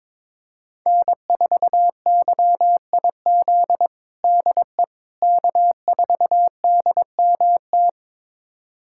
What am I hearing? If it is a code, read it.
Morse code, heard as a keyed tone, about 23 wpm, 700 Hz. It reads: N4YIZ DE K4DMT
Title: USA Callsign DE Callsign - 5 Characters - 22wpm